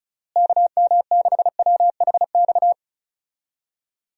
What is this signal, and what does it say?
Morse code, heard as a keyed tone, about 34 wpm, 700 Hz. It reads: KM6WHX
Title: USA Call Signs - 6 Characters - Repeated 3x - 35wpm